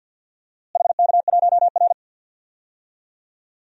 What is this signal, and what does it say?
Morse code, heard as a keyed tone, about 47 wpm, 700 Hz. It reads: HK1R
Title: DXCC-100 Call Signs - 4 Characters - Repeated 3x - 50wpm